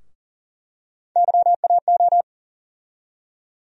Morse code, heard as a keyed tone, about 40 wpm, 700 Hz. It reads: YAO
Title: USA Call Sign Postfixes - 3 Letters 40wpm